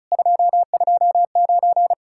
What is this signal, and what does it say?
Morse code, heard as a keyed tone, about 48 wpm, spaced as for 42 wpm, 700 Hz. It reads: 229
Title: Speed-Racing Common QSO Phrases Encore - 35wpm